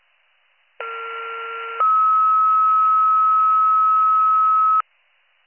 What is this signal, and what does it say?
Two-tone page, heard as a one-second tone, then a three-second tone, about 1370 then 1290 Hz. It is heard from 0.8 to 4.8 s.